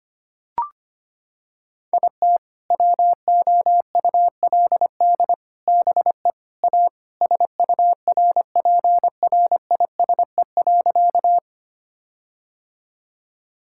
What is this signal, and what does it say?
Morse code, heard as a keyed tone, about 25 wpm, 700 Hz. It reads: IT WOULD BE A SURPRISE.
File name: practice / 25wpm / Sentences with 4-Word Phrases - 25wpm